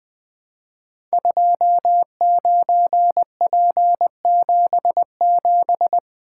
Morse code, heard as a keyed tone, about 20 wpm, 700 Hz. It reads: 29P77
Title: Code Groups - 5 Letter:Numbers - 20wpm